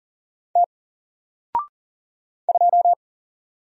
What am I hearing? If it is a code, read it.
Morse code, heard as a keyed tone, about 40 wpm, 700 Hz. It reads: T 2
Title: Single Character (Letter:Number:Punctuation:Prosign) Alt 40wpm